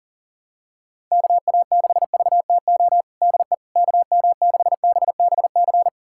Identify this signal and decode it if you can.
Morse code, heard as a keyed tone, about 40 wpm, 700 Hz. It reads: KA6VTO DE KM6BBC
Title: USA Callsign DE Callsign 40wpm